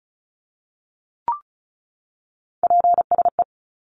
Morse code, heard as a keyed tone, about 34 wpm, 700 Hz. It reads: PSE
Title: Abbreviations 35wpm